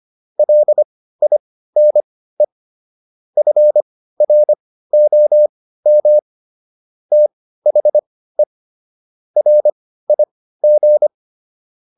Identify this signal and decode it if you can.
Morse code, heard as a keyed tone, about 25 wpm, spaced as for 15 wpm, 600 Hz. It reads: LINE FROM THE RIG